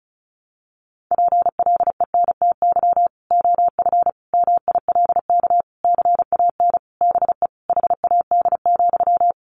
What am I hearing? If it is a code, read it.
Morse code, heard as a keyed tone, about 34 wpm, 700 Hz. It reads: PLENTY OF MILK CAN BE HAD,
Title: Sentences from Top 1500 Words 35wpm